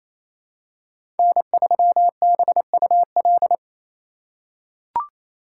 Morse code, heard as a keyed tone, about 28 wpm, 700 Hz. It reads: N3BUL
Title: USA Call Signs - 5 Characters - 2x Character Spacing - 28wpm